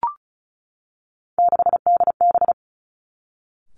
Morse code, heard as a keyed tone, about 36 wpm, 700 Hz. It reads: 6DB
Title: Three Challenging Letter-Number Combinations 35wpm